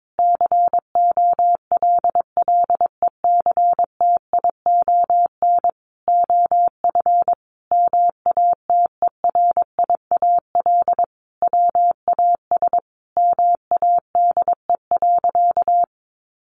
Morse code, heard as a keyed tone, about 22 wpm, 700 Hz. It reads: COLLECTION OF MATERIAL WAS MADE.
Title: Sentences from Top 1200 Words 22wpm